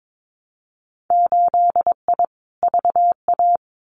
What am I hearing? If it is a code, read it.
Morse code, heard as a keyed tone, about 22 wpm, 700 Hz. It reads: 8I VA
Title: Winter Field Day Exchanges Encore 22wpm